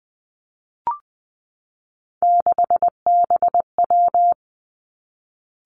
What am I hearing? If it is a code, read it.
Morse code, heard as a keyed tone, about 20 wpm, 700 Hz. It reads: 6BW
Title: Three Challenging Letter-Number Combinations 20wpm